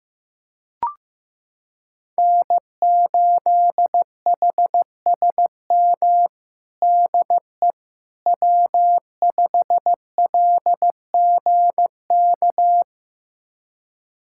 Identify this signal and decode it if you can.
Morse code, heard as a keyed tone, about 16 wpm, 700 Hz. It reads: N8HSM DE W5LGK
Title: USA Callsign DE Callsign - 5 Characters - 15wpm